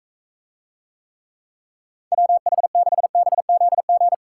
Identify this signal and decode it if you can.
Morse code, heard as a keyed tone, about 44 wpm, 700 Hz. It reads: WH6BZG